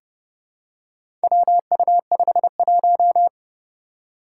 Morse code, heard as a keyed tone, about 29 wpm, 700 Hz. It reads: WU51
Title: Four Challenging Letter-Number Combinations 30wpm